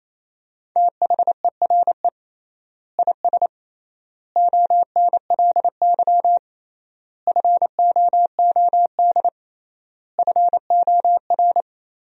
Morse code, heard as a keyed tone, about 27 wpm, 700 Hz. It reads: THERE IS ONLY FOOD FOR
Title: 3x Word-Spacing - Sentences from Top 300 Words - 28wpm